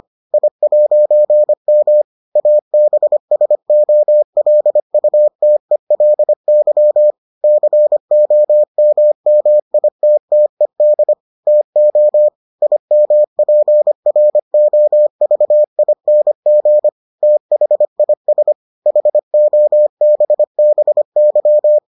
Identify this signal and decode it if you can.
Morse code, heard as a keyed tone, about 25 wpm, 600 Hz. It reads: I'M ABSOLUTELY COMMITTED TO IMPROVING THIS HOBBY